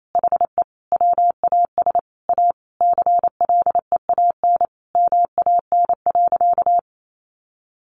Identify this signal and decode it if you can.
Morse code, heard as a keyed tone, about 28 wpm, 700 Hz. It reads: HE WAS A CLEAN MAN.
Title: Sentences from Top 1400 Words 28wpm